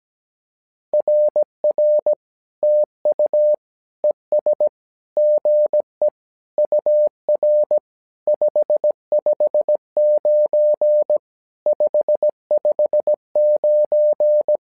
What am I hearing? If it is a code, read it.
Morse code, heard as a keyed tone, about 17 wpm, 600 Hz. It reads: RR TU ES GE UR 559 559
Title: SOTA QSO Practice 17wpm